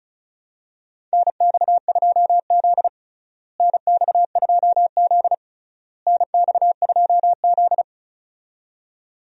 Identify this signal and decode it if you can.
Morse code, heard as a keyed tone, about 36 wpm, 700 Hz. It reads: NX2Z NX2Z NX2Z
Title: USA Call Signs - 4 Characters - Repeated 3x - 35wpm